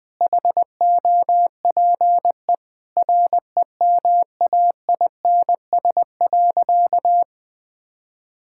Morse code, heard as a keyed tone, about 20 wpm, 700 Hz. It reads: HOPE REMAINS.
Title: Sentences with 2-Word Phrases - 20wpm